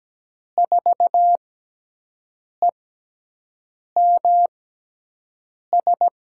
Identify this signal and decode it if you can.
Morse code, heard as a keyed tone, about 17 wpm, spaced as for 6 wpm, 700 Hz. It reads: 4EMS
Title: USA Call Signs - 6 Characters - 6x Character Spacing - 17wpm